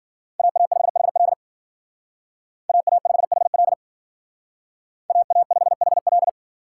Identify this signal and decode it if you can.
Morse code, heard as a keyed tone, about 52 wpm, 700 Hz. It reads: AA5HL AA5HL AA5HL